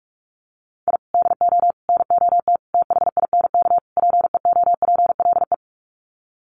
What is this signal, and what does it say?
Morse code, heard as a keyed tone, about 44 wpm, 700 Hz. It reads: I DO NOT THINK PEOPLE
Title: Sets of 5 Words Encore2 45wpm